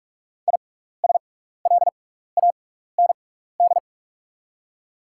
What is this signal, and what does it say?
Morse code, heard as a keyed tone, about 48 wpm, spaced as for 16 wpm, 700 Hz. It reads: ISLAND
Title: 6x Character Spacing - Top 500 Words - 45wpm